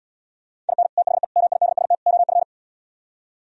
Morse code, heard as a keyed tone, about 77 wpm, 700 Hz. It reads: IN THE MIDST OF